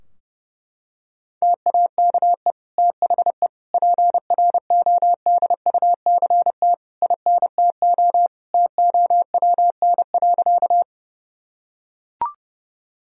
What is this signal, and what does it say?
Morse code, heard as a keyed tone, about 32 wpm, 700 Hz. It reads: TAKE THE PRODUCT INTO TOWN.
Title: Sentences from Top 1000 Words - Review Part 2 - 30wpm